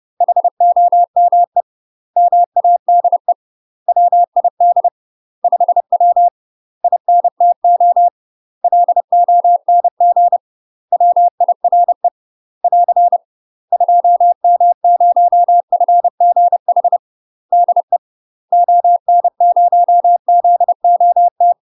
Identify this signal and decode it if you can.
Morse code, heard as a keyed tone, about 30 wpm, 700 Hz. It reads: HOME MADE WID 5W INTO LONG WIRE <AR> 2M0FGH DE ON0ZOT
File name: UK MASTER QSOs TRAINING / 30WPM / UK MASTER QSOs 0028